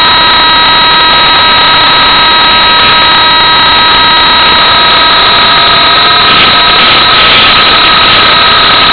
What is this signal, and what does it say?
PSK31, about 31.25 baud.